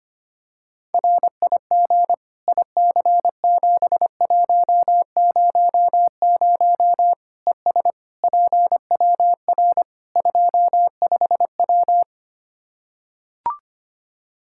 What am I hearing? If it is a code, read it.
Morse code, heard as a keyed tone, about 25 wpm, 700 Hz. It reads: RIG IC7100 ES PWR 25W
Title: K1USN SST QSO Practice 25wpm